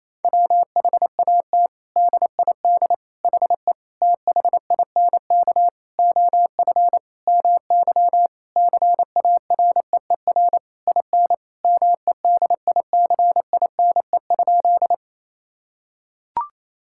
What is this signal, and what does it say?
Morse code, heard as a keyed tone, about 28 wpm, 700 Hz. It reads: WHAT DID HE THINK OF MY CAREER IN MEDICINE?
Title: Sentences from Top 1000 Words - Review Part 6 - 28wpm